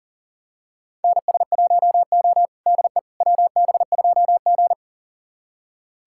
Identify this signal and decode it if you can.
Morse code, heard as a keyed tone, about 37 wpm, 700 Hz. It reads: NS1O DE WB2G
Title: USA Callsign DE Callsign - 4 Characters - 40wpm